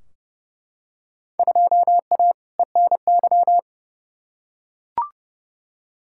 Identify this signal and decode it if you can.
Morse code, heard as a keyed tone, about 30 wpm, 700 Hz. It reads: 2A ENY
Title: Field Day Exchanges Encore - 30wpm